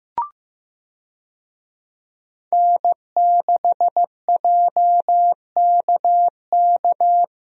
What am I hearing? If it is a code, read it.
Morse code, heard as a keyed tone, about 15 wpm, 700 Hz. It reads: N6JKK
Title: USA Call Signs - 5 Characters - Repeated 3x - 15wpm